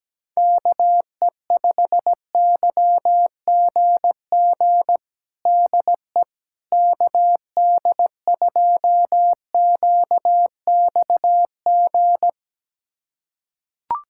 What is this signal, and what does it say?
Morse code, heard as a keyed tone, about 17 wpm, 700 Hz. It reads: KE5YGG DE KD2QXG
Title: USA Callsign DE Callsign - 6 Characters - 17wpm